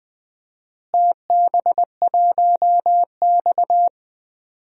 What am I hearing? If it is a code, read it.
Morse code, heard as a keyed tone, about 20 wpm, 700 Hz. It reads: TB1X
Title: DXCC-100 Call Signs - 4 Characters - Repeated 2x - 20wpm